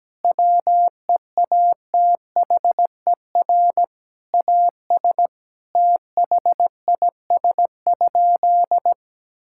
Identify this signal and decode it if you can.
Morse code, heard as a keyed tone, about 17 wpm, 700 Hz. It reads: WEATHER AS THIS?
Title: Sentences from Top 1000 Words - Review Part 5 - 17wpm